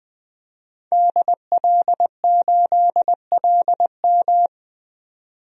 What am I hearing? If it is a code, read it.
Morse code, heard as a keyed tone, about 20 wpm, 700 Hz. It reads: DL8LM
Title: DXCC-100 Call Signs - 5 Characters - 20wpm